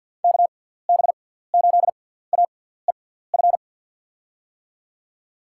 Morse code, heard as a keyed tone, about 48 wpm, spaced as for 17 wpm, 700 Hz. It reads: KB8AEF
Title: USA Call Signs - 6 Characters - 6x Character Spacing - 50wpm